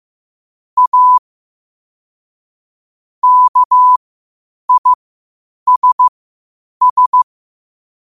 Morse code, heard as a keyed tone, about 15 wpm, spaced as for 8 wpm, 1000 Hz. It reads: A KISS
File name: qotdfiles / 8wpm